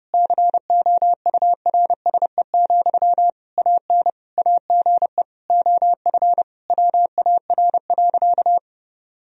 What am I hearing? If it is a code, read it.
Morse code, heard as a keyed tone, about 30 wpm, 700 Hz. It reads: COURSE, AN AGE OF WAR.
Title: Sentences from Top 1000 Words - Review Part 3 - 30wpm